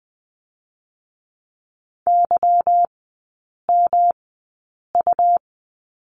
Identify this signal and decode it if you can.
Morse code, heard as a keyed tone, about 20 wpm, spaced as for 8 wpm, 690 Hz. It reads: YMU
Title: 172 Lesson 28e - Number 9 - Three Characters 20wpm